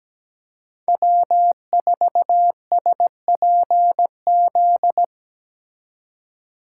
Morse code, heard as a keyed tone, about 17 wpm, 700 Hz. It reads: W4SPZ
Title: USA Call Signs - 4x Character Spacing - 17wpm